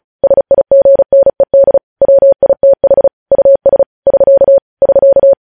Morse code, heard as a keyed tone, about 36 wpm, 550 Hz. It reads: SIGNED WITH US <SK> <SK>
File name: podcast / News headlines-1x-35x35